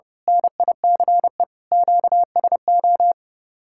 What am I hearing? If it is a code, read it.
Morse code, heard as a keyed tone, about 28 wpm, 700 Hz. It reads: NICE QSO
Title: Speed-Racing Common QSO Phrases - 30wpm